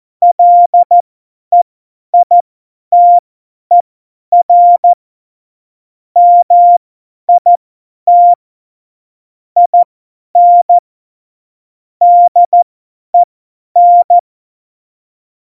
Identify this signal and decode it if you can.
Morse code, heard as a keyed tone, about 14 wpm, spaced as for 10 wpm, 700 Hz. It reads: LEITER MIT IN DEN
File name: Morsekurs / Witze / Witz 25 Juli 17, Tempo 50 F70